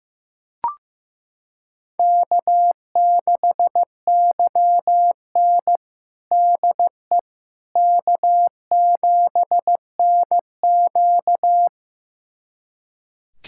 Morse code, heard as a keyed tone, about 15 wpm, 700 Hz. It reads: K6YN DE K7NQ
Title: USA Callsign DE Callsign - 4 Characters - 15wpm